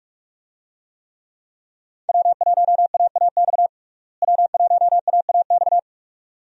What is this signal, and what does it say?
Morse code, heard as a keyed tone, about 46 wpm, 700 Hz. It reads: W1AAX W1AAX
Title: USA Call Signs - 5 Characters - Repeated 2x - 45wpm